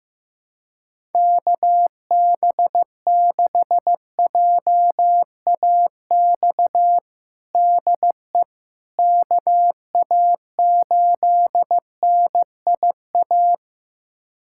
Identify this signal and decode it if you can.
Morse code, heard as a keyed tone, about 15 wpm, 700 Hz. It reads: KB6JAX DE KA8NIA
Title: USA Callsign DE Callsign - 6 Characters - 15wpm